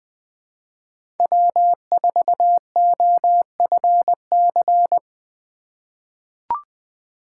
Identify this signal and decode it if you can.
Morse code, heard as a keyed tone, about 20 wpm, 700 Hz. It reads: W4OFC
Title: USA Call Signs - 2x Character Spacing - 20wpm